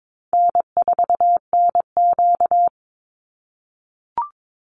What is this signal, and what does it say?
Morse code, heard as a keyed tone, about 22 wpm, 700 Hz. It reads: N4NQ
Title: Easy Call Signs - 22wpm